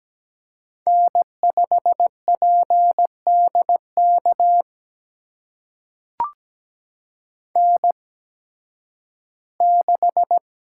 Morse code, heard as a keyed tone, about 17 wpm, spaced as for 11 wpm, 700 Hz. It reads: N5PDK N6
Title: USA Call Signs - 8x Character Spacing - 17wpm